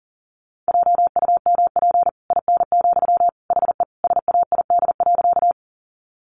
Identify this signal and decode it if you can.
Morse code, heard as a keyed tone, about 40 wpm, 700 Hz. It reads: JUMP IN, HE SAID.
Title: Sentences from Top 1300 Words 40wpm